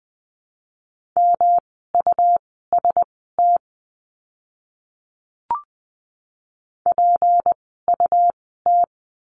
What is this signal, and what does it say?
Morse code, heard as a keyed tone, about 20 wpm, spaced as for 14 wpm, 700 Hz. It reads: MUST PUT
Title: 237 Lesson 38f - Letter q - Words 20wpm